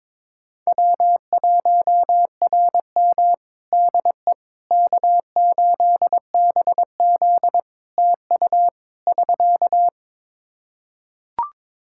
Morse code, heard as a keyed tone, about 22 wpm, 700 Hz. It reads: W1RM DE K8BZ TU <SK>
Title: Practice QSO - Exchange 4 - Ending - 22wpm